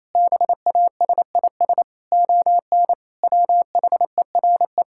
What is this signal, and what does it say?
Morse code, heard as a keyed tone, about 28 wpm, 700 Hz. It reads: BASIS ON WHERE
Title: Sentences from Top 1000 Words - Review Part 5 - 28wpm